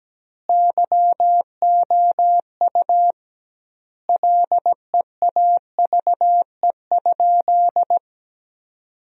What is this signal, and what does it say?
Morse code, heard as a keyed tone, about 17 wpm, 700 Hz. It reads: YOU LEAVE?
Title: 2x Word-Spacing - Sentences from Top 400 Words - 17wpm